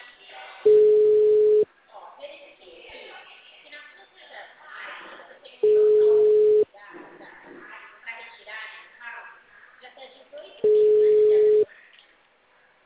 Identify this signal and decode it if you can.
Ringback tone (Europe, 425 Hz), with speech.
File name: out- -2044- - - .35381